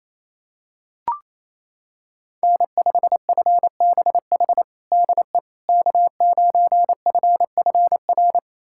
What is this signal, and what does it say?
Morse code, heard as a keyed tone, about 28 wpm, 700 Hz. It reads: N5FBH DE K9FFR
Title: USA Callsign DE Callsign - 5 Characters - 28wpm